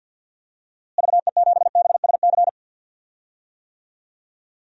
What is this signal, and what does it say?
Morse code, heard as a keyed tone, about 53 wpm, 700 Hz. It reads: VE7BSC
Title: DXCC-100 Call Signs - 6 Characters - Repeated 2x - 50wpm